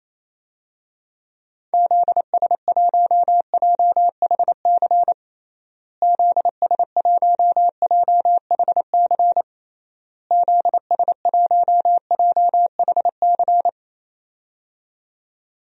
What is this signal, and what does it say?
Morse code, heard as a keyed tone, about 28 wpm, 700 Hz. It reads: ZS1JHC ZS1JHC ZS1JHC
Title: DXCC-100 Call Signs - 6 Characters - Repeated 3x - 28wpm